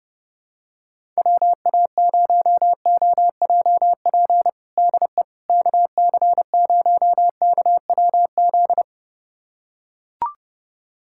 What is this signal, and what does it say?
Morse code, heard as a keyed tone, about 30 wpm, 700 Hz. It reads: WA0OJP DE KC0KWZ
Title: USA Callsign DE Callsign - 6 Characters - 30wpm